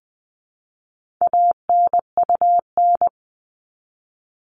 Morse code, heard as a keyed tone, about 20 wpm, 700 Hz. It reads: ANUN
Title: All Cut Numbers - 4-Digit 20wpm